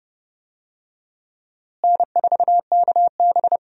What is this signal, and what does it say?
Morse code, heard as a keyed tone, about 30 wpm, 700 Hz. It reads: N4KB
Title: USA Call Signs - 4 Characters - Repeated 2x - 30wpm